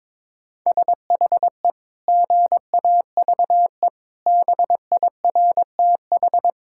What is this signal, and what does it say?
Morse code, heard as a keyed tone, about 22 wpm, 700 Hz. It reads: SHE GAVE BIRTH
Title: Sentences from Top 1000 Words - Review Part 5 - 22wpm